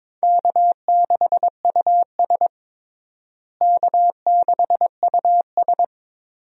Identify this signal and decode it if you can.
Morse code, heard as a keyed tone, about 22 wpm, 700 Hz. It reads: K6US K6US
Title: USA Call Signs - 4 Characters - Repeated 2x - 22wpm